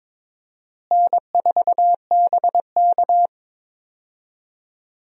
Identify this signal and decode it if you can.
Morse code, heard as a keyed tone, about 22 wpm, 700 Hz. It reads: N4BK
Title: USA Call Signs - 4 Characters - Repeated 2x - 22wpm